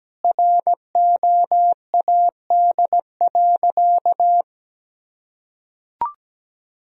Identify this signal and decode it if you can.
Morse code, heard as a keyed tone, about 17 wpm, 700 Hz. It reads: ROAD.
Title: Sentences from Top 400 Words and Abbreviations 17wpm